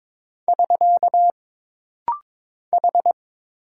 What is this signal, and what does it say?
Morse code, heard as a keyed tone, about 22 wpm, 700 Hz. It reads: <SK> H
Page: This rapid-fire timing is geared to push your ICR (Instant Character Recognition) skills.